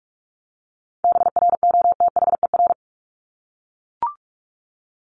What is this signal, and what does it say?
Morse code, heard as a keyed tone, about 44 wpm, 700 Hz. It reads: BROTHER
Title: Top 1751-2000 Words 45wpm